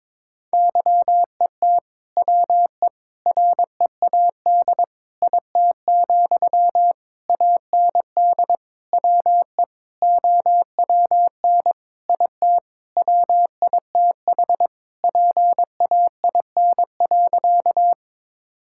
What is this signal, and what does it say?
Morse code, heard as a keyed tone, about 22 wpm, 700 Hz. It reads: YET WE READ IT, AND WE OWN IT WITH PAIN.
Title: Sentences from Top 1600 Words 22wpm